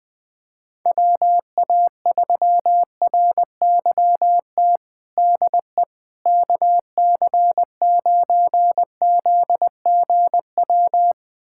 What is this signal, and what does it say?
Morse code, heard as a keyed tone, about 20 wpm, 700 Hz. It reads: WA3RYT DE KC9ZGW
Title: USA Callsign DE Callsign 20wpm